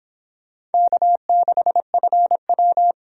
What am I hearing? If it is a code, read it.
Morse code, heard as a keyed tone, about 26 wpm, 700 Hz. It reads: K6FW